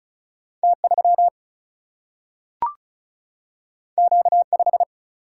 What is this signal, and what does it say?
Morse code, heard as a keyed tone, about 36 wpm, 700 Hz. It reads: T3 Q5
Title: Single-Letter Single-Number 35wpm